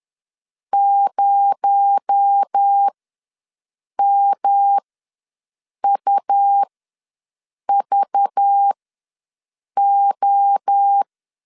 Morse code, heard as a keyed tone, about 10 wpm, spaced as for 6 wpm, 800 Hz. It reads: 0MUVO
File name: LEZIONI / 9 2867 ws 10wpm 4215